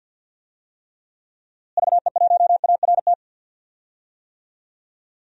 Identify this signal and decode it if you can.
Morse code, heard as a keyed tone, about 51 wpm, 700 Hz. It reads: VE1ART
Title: DXCC-100 Call Signs - 6 Characters - Repeated 3x - 50wpm